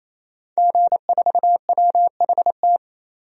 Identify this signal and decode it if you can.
Morse code, heard as a keyed tone, about 28 wpm, 700 Hz. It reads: G4WHT
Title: DXCC-100 Call Signs - 5 Characters - 28wpm